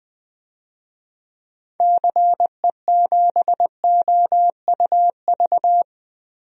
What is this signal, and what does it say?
Morse code, heard as a keyed tone, about 21 wpm, 700 Hz. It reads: CE7OUV
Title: DXCC-100 Call Signs - 6 Characters - Repeated 3x - 20wpm